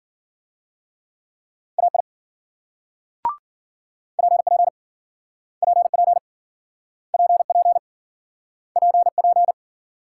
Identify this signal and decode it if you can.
Morse code, heard as a keyed tone, about 51 wpm, 700 Hz. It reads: AI PP PP PP PP